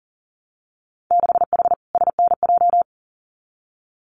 Morse code, heard as a keyed tone, about 41 wpm, 700 Hz. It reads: 6H SNJ
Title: Winter Field Day Exchanges Encore 40wpm